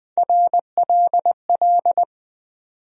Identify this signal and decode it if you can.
Morse code, heard as a keyed tone, about 20 wpm, 700 Hz. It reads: RLL
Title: USA Call Sign Postfixes - 3 Letters 20wpm